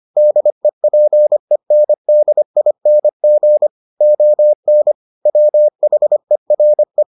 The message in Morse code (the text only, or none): DEPENDING ON WHERE